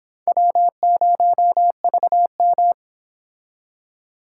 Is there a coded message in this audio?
Yes: Morse code from 0.3 to 2.7 s, heard as a keyed tone, about 29 wpm, 700 Hz. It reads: W0VM